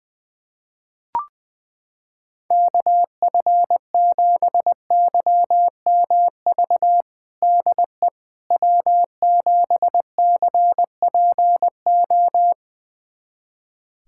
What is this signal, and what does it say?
Morse code, heard as a keyed tone, about 20 wpm, 700 Hz. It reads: KF7YMV DE W7CPO
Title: USA Callsign DE Callsign 20wpm